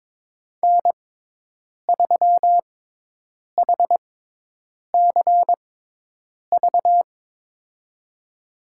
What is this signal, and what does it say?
Morse code, heard as a keyed tone, about 22 wpm, spaced as for 8 wpm, 700 Hz. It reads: N3HCV
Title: USA Call Signs - 5 Characters - 6x Character Spacing - 22wpm